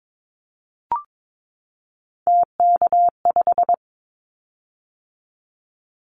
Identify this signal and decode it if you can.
Morse code, heard as a keyed tone, about 22 wpm, 700 Hz. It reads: TK5
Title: Callsign Prefixes for Top 100 DXCC - 22wpm